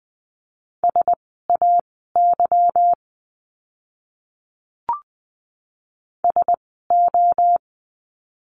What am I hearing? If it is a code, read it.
Morse code, heard as a keyed tone, about 20 wpm, spaced as for 14 wpm, 700 Hz. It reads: SAY SO
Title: 230 Lesson 37f - Letter x - Words 20wpm